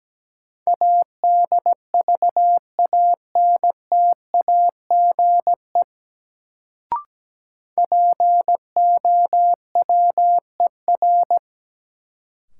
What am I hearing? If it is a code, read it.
Morse code, heard as a keyed tone, about 17 wpm, 700 Hz. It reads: ADVANTAGE POWER
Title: Sets of 4 Words - New Word Drill - 17wpm